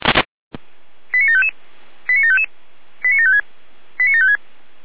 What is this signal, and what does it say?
Five-tone selective call (ZVEI), heard as quick runs of five tones, tones of about 70 ms. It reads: 89755 89755 89756 89756